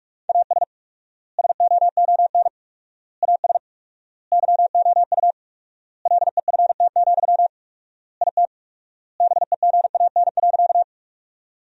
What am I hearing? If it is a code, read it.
Morse code, heard as a keyed tone, about 44 wpm, 700 Hz. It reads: AS SOON AS YOU LEFT, IT BEGAN.
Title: 4x Word-Spacing - Sentences from Top 400 Words - 45wpm